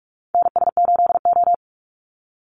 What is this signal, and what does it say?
Morse code, heard as a keyed tone, about 44 wpm, 700 Hz. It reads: NS8O
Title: Call Signs - 45wpm